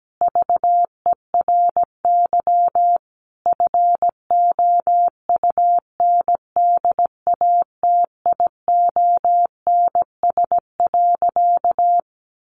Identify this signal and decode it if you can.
Morse code, heard as a keyed tone, about 17 wpm, 700 Hz. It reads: VERY FOUNDATIONS.
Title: A Sound Of Thunder 17wpm